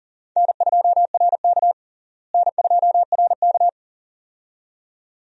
Morse code, heard as a keyed tone, about 40 wpm, 700 Hz. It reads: N2RK N2RK
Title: USA Call Signs - 4 Characters - Repeated 2x - 40wpm